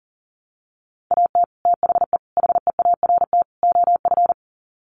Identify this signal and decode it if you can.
Morse code, heard as a keyed tone, about 40 wpm, 700 Hz. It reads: AT THE HEART OF
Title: Sets of 4 Words 40wpm